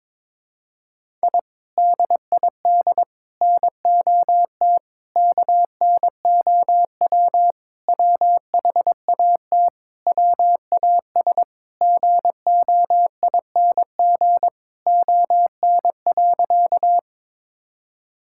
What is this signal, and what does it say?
Morse code, heard as a keyed tone, about 22 wpm, 700 Hz. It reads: I DID NOT KNOW WHAT WAS GOING ON.
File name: practice / 22wpm / Sentences with 4-Word Phrases - 22wpm